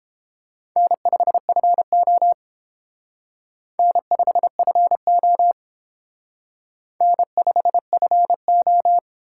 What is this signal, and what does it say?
Morse code, heard as a keyed tone, about 30 wpm, 700 Hz. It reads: N5FO N5FO N5FO